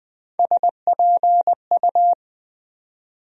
Morse code, heard as a keyed tone, about 20 wpm, 700 Hz. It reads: SPU
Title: USA Call Sign Postfixes - 3 Letters 20wpm